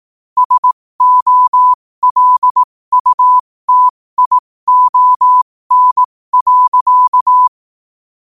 Morse code, heard as a keyed tone, about 18 wpm, spaced as for 15 wpm, 1000 Hz. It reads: SOLUTION.
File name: qotdfiles / 15wpm